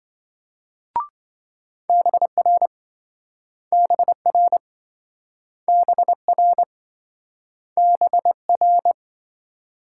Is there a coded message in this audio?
Yes: Morse code from 1.9 to 8.9 s, heard as a keyed tone, about 27 wpm, 700 Hz. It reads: BR BR BR BR